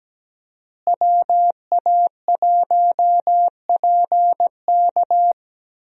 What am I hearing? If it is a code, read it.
Morse code, heard as a keyed tone, about 17 wpm, 700 Hz. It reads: WA1PK
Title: USA Call Signs - 5 Characters - Repeated 2x - 17wpm